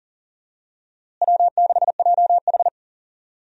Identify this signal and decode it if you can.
Morse code, heard as a keyed tone, about 40 wpm, 700 Hz. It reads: W6JH
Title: USA Call Signs - 4 Characters - Repeated 3x - 40wpm